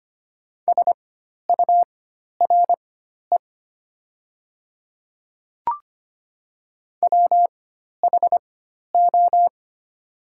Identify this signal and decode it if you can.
Morse code, heard as a keyed tone, about 25 wpm, spaced as for 12 wpm, 700 Hz. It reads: SURE WHO
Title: 4x Character Spacing - Top 500 Words - 25wpm